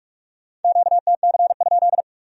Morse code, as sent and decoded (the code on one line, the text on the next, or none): --.- - -.-. ..--..
QTC?